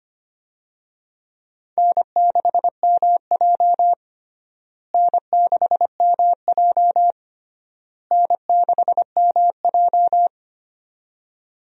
Morse code, heard as a keyed tone, about 25 wpm, 700 Hz. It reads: N6MJ N6MJ N6MJ
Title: USA Call Signs - 4 Characters - Repeated 3x - 25wpm